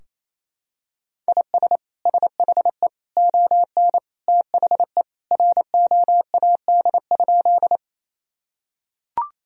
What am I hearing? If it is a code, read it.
Morse code, heard as a keyed tone, about 28 wpm, 700 Hz. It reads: IS SHE ON THE ROAD?
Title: Sentences from Top 1000 Words - Review Part 2 - 28wpm